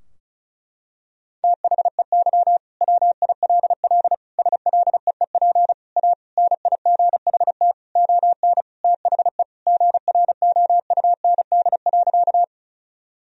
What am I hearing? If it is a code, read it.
Morse code, heard as a keyed tone, about 34 wpm, 700 Hz. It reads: THEY WILL SLEEP A NIGHT ON THE GROUND.
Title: Sentences from Top 1000 Words - Review Part 2 - 35wpm